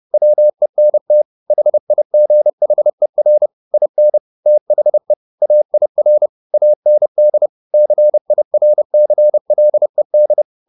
Morse code, heard as a keyed tone, about 30 wpm, 600 Hz. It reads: WENT HIGHER IN THE AIR AND CIRCLED
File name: the old man and the sea / the old man and the sea 1 30wpm